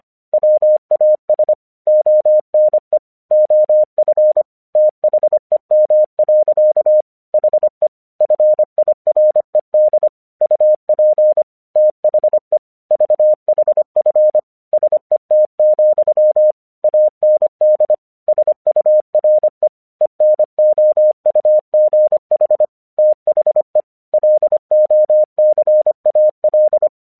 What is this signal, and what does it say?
Morse code, heard as a keyed tone, about 24 wpm, 610 Hz. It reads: WAS ONE OF THEM. HE FIRED UP THE VHF SET, AND SURE ENOUGH THE LOCAL
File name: 1 English 25wpm / AQC05